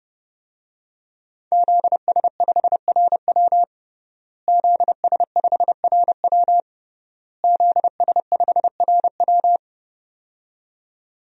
Morse code, heard as a keyed tone, about 30 wpm, 700 Hz. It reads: ZS5RW ZS5RW ZS5RW
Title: DXCC-100 Call Signs - 5 Characters - Repeated 3x - 30wpm